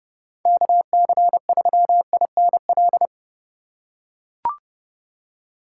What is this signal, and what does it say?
Morse code, heard as a keyed tone, about 30 wpm, 700 Hz. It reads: KC3INL
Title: USA Call Signs - 6 Characters - 3x Character Spacing - 30wpm